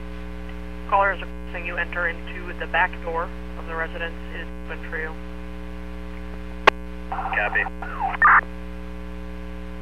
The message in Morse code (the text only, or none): none